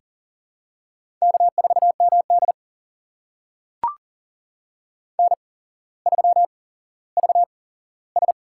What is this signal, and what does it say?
Morse code, heard as a keyed tone, about 40 wpm, spaced as for 17 wpm, 700 Hz. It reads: K4MD N3VS
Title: USA Call Signs - 4 Characters - 8x Character Spacing - 40wpm